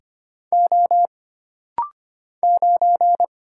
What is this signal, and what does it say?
Morse code, heard as a keyed tone, about 25 wpm, 700 Hz. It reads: O 9
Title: Single Letter-Number - Rapid-Fire 25wpm